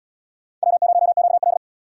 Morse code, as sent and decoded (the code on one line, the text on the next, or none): .-- ----- --.- .-..
W0QL